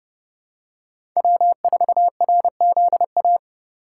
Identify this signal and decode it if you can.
Morse code, heard as a keyed tone, about 30 wpm, 700 Hz. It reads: W4RZA
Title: USA Call Signs - 5 Characters - 3x Character Spacing - 30wpm